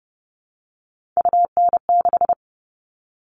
Morse code, heard as a keyed tone, about 30 wpm, 700 Hz. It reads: UN6
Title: All Cut Numbers - 3-Digit 30wpm